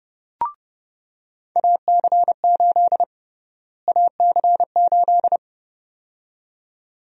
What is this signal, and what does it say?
Morse code, heard as a keyed tone, about 30 wpm, 700 Hz. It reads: AC8 AC8
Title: USA Call Sign Prefixes - Repeated 2x - 30wpm